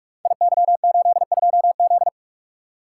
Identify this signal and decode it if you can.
Morse code, heard as a keyed tone, about 42 wpm, 700 Hz. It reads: IY82Z
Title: Code Groups - 5 Letter:Numbers - 45wpm